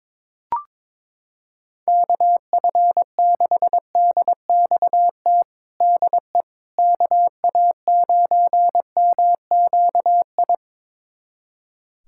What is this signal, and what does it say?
Morse code, heard as a keyed tone, about 22 wpm, 700 Hz. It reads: KF6DXT DE KA9MQI
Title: USA Callsign DE Callsign - 6 Characters - 22wpm